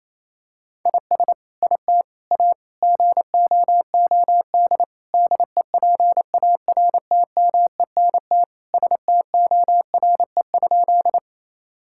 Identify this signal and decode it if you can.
Morse code, heard as a keyed tone, about 28 wpm, 700 Hz. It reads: IS IT A GOOD DEPARTMENT STORE?
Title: Sentences from Top 1000 Words - Review Part 5 - 28wpm